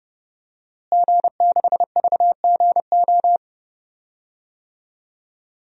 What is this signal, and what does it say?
Morse code, heard as a keyed tone, about 30 wpm, 700 Hz. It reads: G6VGO
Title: DXCC-100 Call Signs - 5 Characters - Repeated 2x - 30wpm